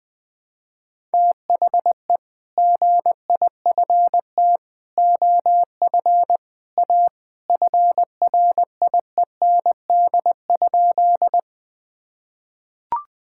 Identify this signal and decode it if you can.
Morse code, heard as a keyed tone, about 19 wpm, 700 Hz. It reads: THE GIFT OF A FRIEND?
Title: Sentences from Top 1000 Words - Review Part 6 - 20wpm